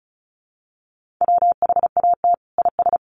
Morse code, heard as a keyed tone, about 34 wpm, 700 Hz. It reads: WHAT IS
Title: Sentences from Top 800 Words 35wpm